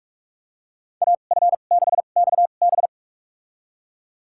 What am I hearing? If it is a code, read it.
Morse code, heard as a keyed tone, about 46 wpm, spaced as for 32 wpm, 700 Hz. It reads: AF6XB
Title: USA Call Signs - 5 Characters - 2x Character Spacing - 45wpm